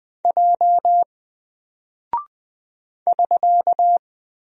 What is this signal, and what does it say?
Morse code, heard as a keyed tone, about 20 wpm, 700 Hz. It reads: J <SK>
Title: Single Character (Letter:Number:Punctuation:Prosign) 20wpm